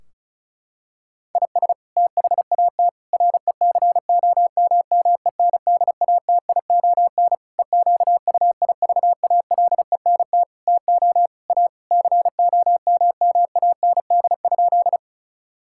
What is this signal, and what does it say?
Morse code, heard as a keyed tone, about 34 wpm, 700 Hz. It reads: IS THAT RECOMMENDATION EQUIVALENT TO A COMMAND?
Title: Sentences from Top 1600 Words 35wpm